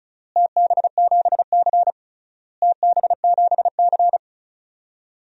Morse code, heard as a keyed tone, about 36 wpm, 700 Hz. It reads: TB7C TB7C
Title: DXCC-100 Call Signs - 4 Characters - Repeated 3x - 35wpm